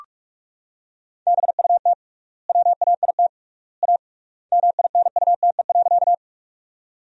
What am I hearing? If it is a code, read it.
Morse code, heard as a keyed tone, about 46 wpm, 700 Hz. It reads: BUT WAIT A MINUTE.
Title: 3x Word-Spacing - Sentences from Top 500 Words - 45wpm